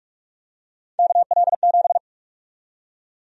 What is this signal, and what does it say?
Morse code, heard as a keyed tone, about 47 wpm, 700 Hz. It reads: KL7
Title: USA Call Sign Prefixes - 45wpm